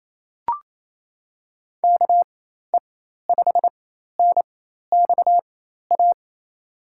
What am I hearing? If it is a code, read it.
Morse code, heard as a keyed tone, about 28 wpm, spaced as for 13 wpm, 700 Hz. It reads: KE5NXA
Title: USA Call Signs - 6 Characters - 4x Character Spacing - 28wpm